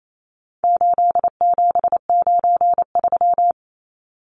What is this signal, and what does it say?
Morse code, heard as a keyed tone, about 28 wpm, 700 Hz. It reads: 8793
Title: Numbers - 4-Digit 28wpm